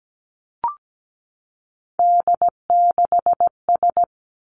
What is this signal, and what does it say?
Morse code, heard as a keyed tone, about 17 wpm, 700 Hz. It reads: D6S
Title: Three Challenging Letter-Number Combinations 17wpm